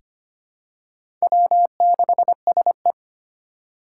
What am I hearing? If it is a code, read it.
Morse code, heard as a keyed tone, about 25 wpm, 700 Hz. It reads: W6SE
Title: USA Call Signs - 4 Characters - 2x Character Spacing - 25wpm